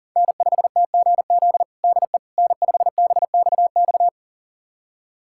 Morse code, heard as a keyed tone, about 40 wpm, 700 Hz. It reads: N5TGZ DE N5BXX
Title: USA Callsign DE Callsign - 5 Characters - 40wpm